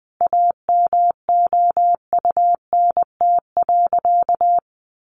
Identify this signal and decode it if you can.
Morse code, heard as a keyed tone, about 20 wpm, 700 Hz. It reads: AMOUNT.
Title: Sentences from Top 700 Words 20wpm